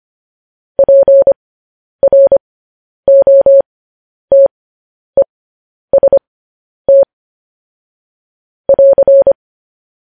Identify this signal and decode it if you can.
Morse code, heard as a keyed tone, about 25 wpm, spaced as for 10 wpm, 550 Hz. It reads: PROTEST <AR>
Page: news headlines in morse code at 10 wpm